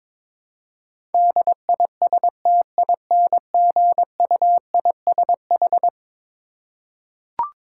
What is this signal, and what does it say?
Morse code, heard as a keyed tone, about 22 wpm, 700 Hz. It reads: DISTINGUISH
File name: practice / 22wpm / Eleven Letter Words - 22wpm